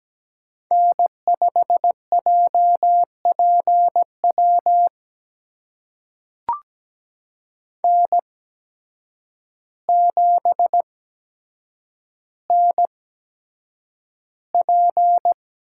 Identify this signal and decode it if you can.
Morse code, heard as a keyed tone, about 17 wpm, spaced as for 8 wpm, 700 Hz. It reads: N5JPW N7NP
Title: USA Call Signs - 8x Character Spacing - 17wpm